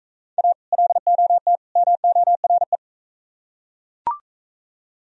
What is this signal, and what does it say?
Morse code, heard as a keyed tone, about 42 wpm, 700 Hz. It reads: A LOT MORE